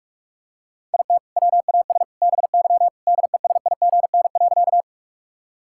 Morse code, heard as a keyed tone, about 46 wpm, 700 Hz. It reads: IT WAS BY DESIGN.
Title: Sentences from Top 1000 Words - Review Part 5 - 45wpm